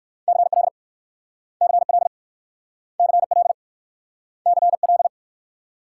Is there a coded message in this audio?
Yes: Morse code from 0.3 to 5.1 s, heard as a keyed tone, about 59 wpm, 700 Hz. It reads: CL CL CL CL